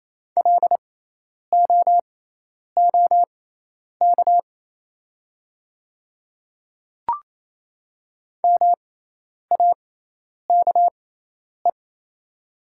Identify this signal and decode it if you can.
Morse code, heard as a keyed tone, about 28 wpm, spaced as for 10 wpm, 700 Hz. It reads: LOOK MAKE
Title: 6x Character Spacing - Top 100 Words - 28wpm